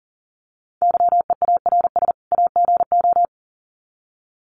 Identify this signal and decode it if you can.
Morse code, heard as a keyed tone, about 40 wpm, 700 Hz. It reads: YEARS AGO
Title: Sets of 2 Words 40wpm